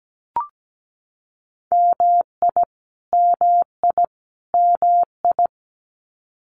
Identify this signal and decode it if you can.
Morse code, heard as a keyed tone, about 17 wpm, 700 Hz. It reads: MI MI MI
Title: US States Repeated 3x 17wpm